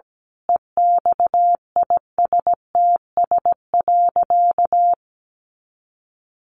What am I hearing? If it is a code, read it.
Morse code, heard as a keyed tone, about 17 wpm, 700 Hz. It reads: EXISTS.
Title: A Sound Of Thunder 17wpm